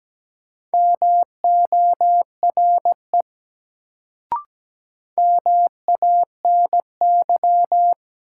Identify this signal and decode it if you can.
Morse code, heard as a keyed tone, about 17 wpm, 700 Hz. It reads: MORE MANY
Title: Sets of 2 Words - New Word Drill - 17wpm